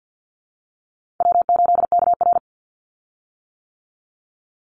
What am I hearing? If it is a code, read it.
Morse code, heard as a keyed tone, about 50 wpm, 700 Hz. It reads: W8KR
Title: Call Signs - 50wpm